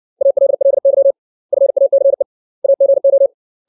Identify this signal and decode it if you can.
Morse code, heard as a keyed tone, about 60 wpm, 550 Hz. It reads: ALLY FACE AGO
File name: 632 words / 632 Words 60WPM